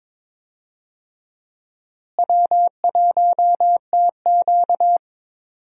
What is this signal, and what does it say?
Morse code, heard as a keyed tone, about 22 wpm, 700 Hz. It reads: W1TQ
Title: USA Call Signs - 4 Characters - Repeated 2x - 22wpm